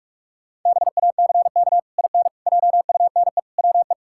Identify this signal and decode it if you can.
Morse code, heard as a keyed tone, about 46 wpm, 700 Hz. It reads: BACK IN JUNE WE
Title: Pangram Challenge v2 - 45wpm